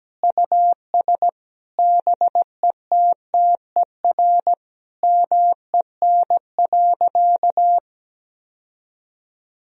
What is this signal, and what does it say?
Morse code, heard as a keyed tone, about 17 wpm, 700 Hz. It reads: US BETTER MEN.
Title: Sentences from Top 1000 Words - Review Part 3 - 17wpm